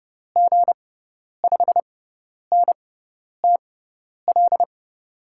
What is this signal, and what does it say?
Morse code, heard as a keyed tone, about 31 wpm, spaced as for 10 wpm, 700 Hz. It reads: G5NTL
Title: USA Call Signs - 6 Characters - 6x Character Spacing - 30wpm